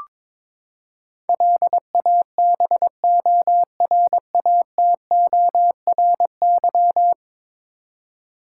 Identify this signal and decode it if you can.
Morse code, heard as a keyed tone, about 22 wpm, 700 Hz. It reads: LABORATORY
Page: morse-code-ninja-practice-22wpm